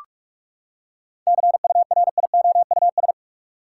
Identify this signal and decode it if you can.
Morse code, heard as a keyed tone, about 46 wpm, 700 Hz. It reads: CURIOUS
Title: Seven Letter Words - 45wpm